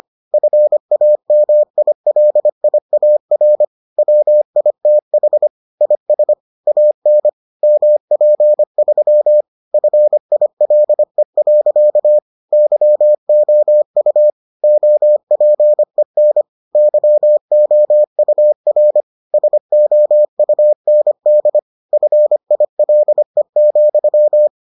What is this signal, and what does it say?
Morse code, heard as a keyed tone, about 25 wpm, 600 Hz. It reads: FAMILIAR WITH IS AN MP3 FILE. YOU OPEN YOUR SOUND FILE,